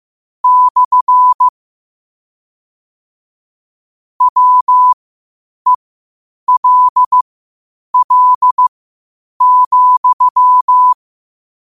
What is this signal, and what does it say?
Morse code, heard as a keyed tone, about 15 wpm, spaced as for 8 wpm, 1000 Hz. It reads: / WELL,
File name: qotdfiles / 8wpm